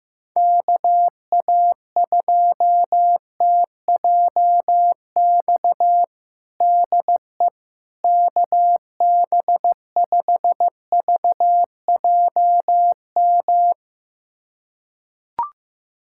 Morse code, heard as a keyed tone, about 15 wpm, 700 Hz. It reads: KA2TJX DE KB5VJM
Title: USA Callsign DE Callsign 15wpm